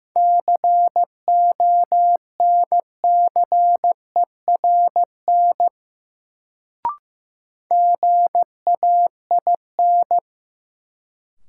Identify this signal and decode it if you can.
Morse code, heard as a keyed tone, about 15 wpm, 700 Hz. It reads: CONCERN GAIN
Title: Sentences from Top 1000 Words - New Word Drill - 15wpm